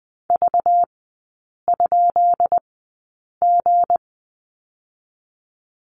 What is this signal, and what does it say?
Morse code, heard as a keyed tone, about 20 wpm, spaced as for 8 wpm, 690 Hz. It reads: V?G
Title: 191 Lesson 31e - Letter v - Three Characters 20wpm